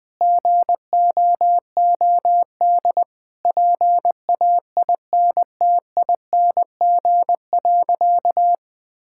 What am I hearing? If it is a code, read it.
Morse code, heard as a keyed tone, about 21 wpm, 700 Hz. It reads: GOOD PAINTING.
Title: Sentences from Top 1000 Words - Review Part 6 - 20wpm